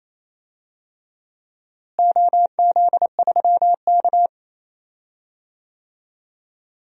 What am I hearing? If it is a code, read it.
Morse code, heard as a keyed tone, about 28 wpm, 700 Hz. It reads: OZ3K
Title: DXCC-100 Call Signs - 4 Characters - Repeated 2x - 28wpm